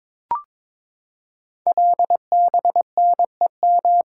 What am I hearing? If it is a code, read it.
Morse code, heard as a keyed tone, about 22 wpm, 700 Hz. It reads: LBNEM
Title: Code Groups - 5 Letters - 22wpm